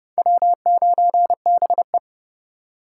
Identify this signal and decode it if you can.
Morse code, heard as a keyed tone, about 40 wpm, spaced as for 35 wpm, 700 Hz. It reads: W9BE
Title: Speed Racing - USA 1x2 Call Signs 30wpm